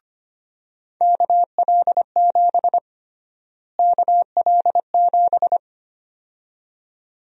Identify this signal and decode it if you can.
Morse code, heard as a keyed tone, about 25 wpm, 700 Hz. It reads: KL7 KL7
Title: USA Call Sign Prefixes - Repeated 2x - 25wpm